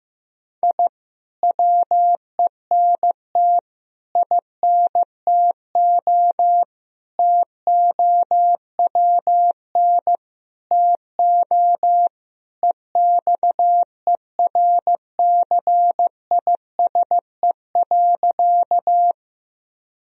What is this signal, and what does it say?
Morse code, heard as a keyed tone, about 16 wpm, 700 Hz. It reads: I WENT INTO TOWN TO EXERCISE.
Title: Sentences from Top 1000 Words - Review Part 4 - 15wpm